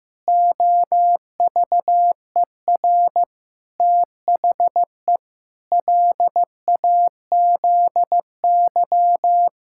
Morse code, heard as a keyed tone, about 15 wpm, 700 Hz. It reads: OVER THE LAZY
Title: Pangram Challenge v2 - 15wpm